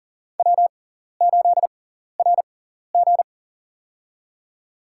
Morse code, heard as a keyed tone, about 40 wpm, spaced as for 14 wpm, 700 Hz. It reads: W8RG
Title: USA Call Signs - 6x Character Spacing - 40wpm